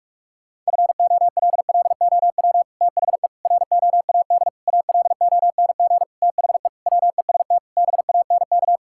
Morse code, heard as a keyed tone, about 41 wpm, 700 Hz. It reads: FOLLOW THE ROAD ALONG THE WEST BANK
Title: Sentences from Top 1000 Words - Review Part 6 - 45wpm